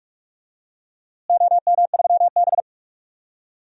Morse code, heard as a keyed tone, about 46 wpm, 700 Hz. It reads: OM3B
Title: DXCC-100 Call Signs - 4 Characters - Repeated 3x - 45wpm